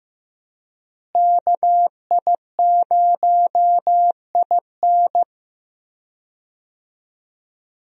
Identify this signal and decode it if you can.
Morse code, heard as a keyed tone, about 15 wpm, 700 Hz. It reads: KI0IN
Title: USA Call Signs - 5 Characters - Repeated 2x - 15wpm